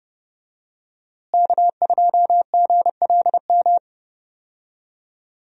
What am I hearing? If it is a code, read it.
Morse code, heard as a keyed tone, about 30 wpm, 700 Hz. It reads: K2GLM
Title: USA Call Signs - 5 Characters - Repeated 3x - 30wpm